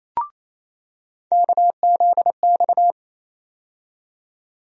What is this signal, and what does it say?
Morse code, heard as a keyed tone, about 28 wpm, 700 Hz. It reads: KZX
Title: USA Call Sign Postfixes - 3 Letters 28wpm